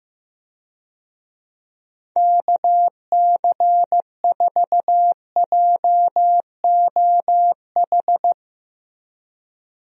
Morse code, heard as a keyed tone, about 15 wpm, 700 Hz. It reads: KC4JOH
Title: USA Call Signs - 6 Characters - Repeated 2x - 15wpm